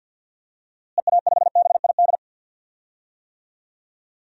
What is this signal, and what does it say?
Morse code, heard as a keyed tone, about 50 wpm, 700 Hz. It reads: EA5BID
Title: DXCC-100 Call Signs - 6 Characters - Repeated 2x - 50wpm